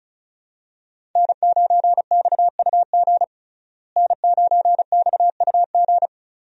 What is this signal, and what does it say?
Morse code, heard as a keyed tone, about 34 wpm, 700 Hz. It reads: N9XUG N9XUG
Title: USA Call Signs - 5 Characters - Repeated 3x - 35wpm